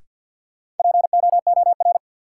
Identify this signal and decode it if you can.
Morse code, heard as a keyed tone, about 52 wpm, 700 Hz. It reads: POOR
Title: Sentences from Top 1000 Words - New Word Drill - 50wpm